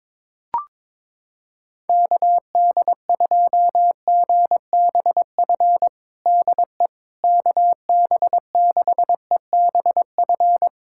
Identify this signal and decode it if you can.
Morse code, heard as a keyed tone, about 21 wpm, 700 Hz. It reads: KD2GBF DE KB6EBF
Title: USA Callsign DE Callsign 22wpm